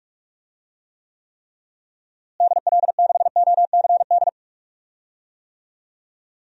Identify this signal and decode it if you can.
Morse code, heard as a keyed tone, about 46 wpm, 700 Hz. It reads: DL6OCD
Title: DXCC-100 Call Signs - 6 Characters - Repeated 3x - 45wpm